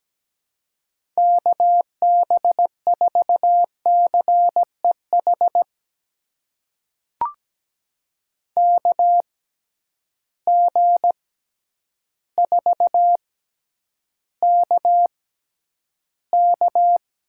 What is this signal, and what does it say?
Morse code, heard as a keyed tone, about 17 wpm, spaced as for 9 wpm, 700 Hz. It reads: KB4CEH KG4KK
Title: USA Call Signs - 6 Characters - 6x Character Spacing - 17wpm